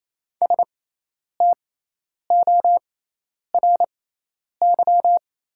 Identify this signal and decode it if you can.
Morse code, heard as a keyed tone, about 28 wpm, spaced as for 10 wpm, 700 Hz. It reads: STORY
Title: 6x Character Spacing - Top 500 Words - 28wpm